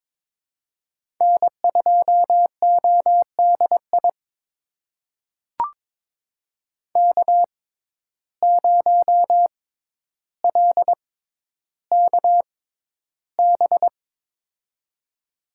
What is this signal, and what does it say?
Morse code, heard as a keyed tone, about 22 wpm, spaced as for 11 wpm, 700 Hz. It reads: N2ODI K0LKB
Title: USA Call Signs - 5 Characters - 6x Character Spacing - 22wpm